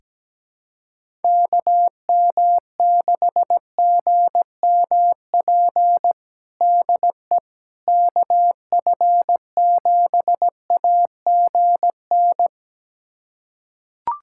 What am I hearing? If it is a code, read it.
Morse code, heard as a keyed tone, about 17 wpm, 700 Hz. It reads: KM6GMP DE KF7AGN
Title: USA Callsign DE Callsign - 6 Characters - 17wpm